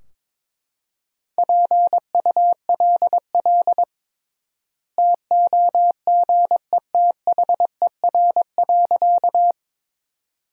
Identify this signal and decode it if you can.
Morse code, heard as a keyed tone, about 22 wpm, 700 Hz. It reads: PULL TOGETHER.
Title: 3x Word-Spacing - Sentences from Top 400 Words - 22wpm